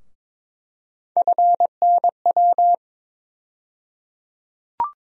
Morse code, heard as a keyed tone, about 22 wpm, 700 Hz. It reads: FNW
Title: USA Call Sign Postfixes - 3 Letters 22wpm